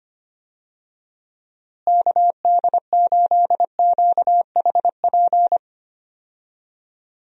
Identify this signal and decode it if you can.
Morse code, heard as a keyed tone, about 25 wpm, 700 Hz. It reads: KD8QHP
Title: USA Call Signs - 6 Characters - Repeated 2x - 25wpm